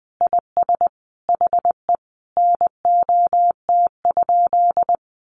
Morse code, heard as a keyed tone, about 20 wpm, 700 Hz. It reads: IS HE NOT?
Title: Sentences from Top 1500 Words 20wpm